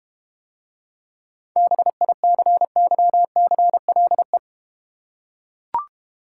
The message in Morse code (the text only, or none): BICYCLE